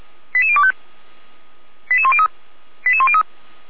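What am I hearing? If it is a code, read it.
Five-tone selective call (ZVEI), heard as quick runs of five tones, tones of about 70 ms. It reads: 88025 88103 88103
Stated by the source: ZVEI five-tone selective call received on an SDR by the Feuerwehr Orlamünde server.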